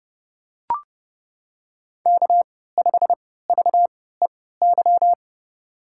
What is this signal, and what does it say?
Morse code, heard as a keyed tone, about 32 wpm, spaced as for 17 wpm, 700 Hz. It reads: K5VEY
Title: USA Call Signs - 3x Character Spacing - 30wpm